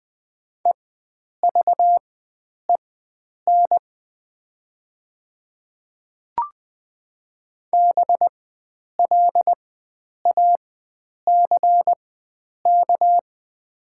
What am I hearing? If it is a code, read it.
Morse code, heard as a keyed tone, about 21 wpm, spaced as for 9 wpm, 700 Hz. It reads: EVEN BLACK
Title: 4x Character Spacing - Top 500 Words - 20wpm